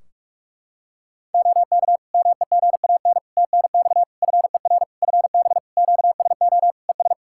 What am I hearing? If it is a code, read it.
Morse code, heard as a keyed tone, about 44 wpm, 700 Hz. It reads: OK MEGAN TNX FER FB QSO ES
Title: Practice QSO - Exchange 4 - Ending - 45wpm